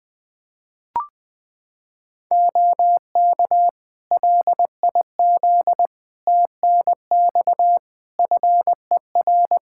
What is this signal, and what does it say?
Morse code, heard as a keyed tone, about 20 wpm, 700 Hz. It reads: OK LIZ TNX FER
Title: Practice QSO - Exchange 4 - Ending - 20wpm